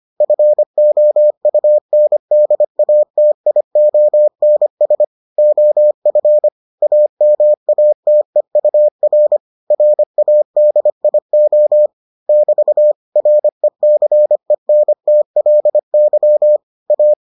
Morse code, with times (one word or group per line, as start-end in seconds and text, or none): 0.2-5.0 FOUNDATIONS
5.4-6.5 OF
6.8-9.4 AMATEUR
9.7-11.9 RADIO
12.3-12.9 <BT>
13.2-16.6 RECENTLY
16.9-17.1 A